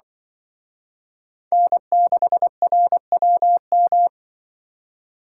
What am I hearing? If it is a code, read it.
Morse code, heard as a keyed tone, about 24 wpm, 700 Hz. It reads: N6RWM
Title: Speed Racing - USA 1x3 Call Signs 20wpm